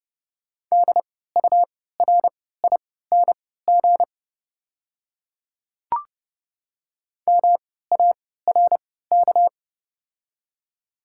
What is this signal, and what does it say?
Morse code, heard as a keyed tone, about 31 wpm, spaced as for 17 wpm, 700 Hz. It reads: DURING MARK
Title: 3x Character Spacing - Top 500 Words - 30wpm